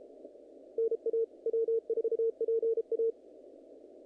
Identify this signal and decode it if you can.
Morse code, heard as a keyed tone, about 33 wpm, 450 Hz. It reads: NA W4PA
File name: contests / naqp-cw-k5tr-op-k5pi- -2000